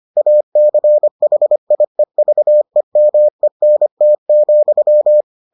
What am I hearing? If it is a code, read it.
Morse code, heard as a keyed tone, about 25 wpm, 600 Hz. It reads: ACHIEVEMENT,